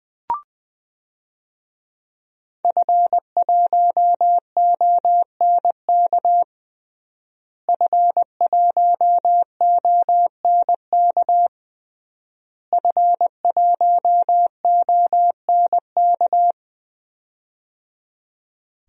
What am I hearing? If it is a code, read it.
Morse code, heard as a keyed tone, about 20 wpm, 700 Hz. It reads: F1ONK F1ONK F1ONK
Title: DXCC-100 Call Signs - 5 Characters - Repeated 3x - 20wpm